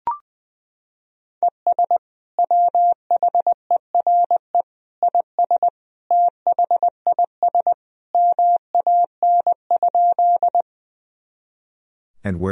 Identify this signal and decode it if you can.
Morse code, heard as a keyed tone, about 20 wpm, 700 Hz. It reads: ES WHERE IS THIS MAN?
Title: Sentences from Top 200 Words and Abbreviations 20wpm